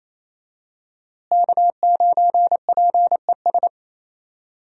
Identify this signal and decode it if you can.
Morse code, heard as a keyed tone, about 28 wpm, 700 Hz. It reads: K9PES
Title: USA Call Signs - 5 Characters - Repeated 2x - 28wpm